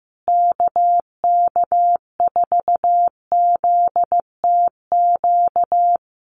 Morse code, heard as a keyed tone, about 14 wpm, 700 Hz. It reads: KK4ZTQ
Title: USA Call Signs Repeated 3x 15wpm